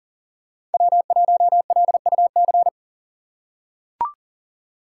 Morse code, heard as a keyed tone, about 40 wpm, 700 Hz. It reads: W1LUC
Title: USA Call Signs - 5 Characters - 3x Character Spacing - 40wpm